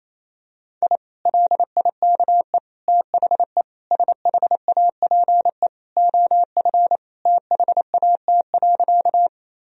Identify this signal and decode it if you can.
Morse code, heard as a keyed tone, about 28 wpm, 700 Hz. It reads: I LIKE THE SHAPE OF THAT.
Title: Sentences from Top 1000 Words - Review Part 3 - 28wpm